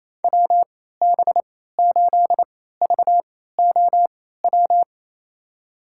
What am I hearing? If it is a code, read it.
Morse code, heard as a keyed tone, about 27 wpm, spaced as for 16 wpm, 700 Hz. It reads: WB8VOW
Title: USA Call Signs - 6 Characters - 3x Character Spacing - 28wpm